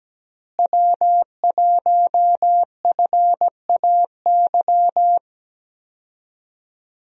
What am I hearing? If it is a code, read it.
Morse code, heard as a keyed tone, about 17 wpm, 700 Hz. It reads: W1FAY
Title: USA Call Signs - 5 Characters - Repeated 2x - 17wpm